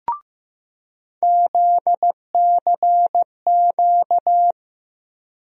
Morse code, heard as a keyed tone, about 15 wpm, 700 Hz. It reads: ZCQ
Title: USA Call Sign Postfixes - 3 Letters 15wpm